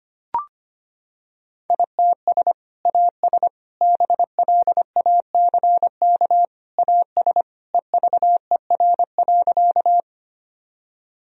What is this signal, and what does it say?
Morse code, heard as a keyed tone, about 25 wpm, 700 Hz. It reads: ITS AS BLACK AS EVER.
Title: Sentences from Top 1000 Words - Review Part 2 - 25wpm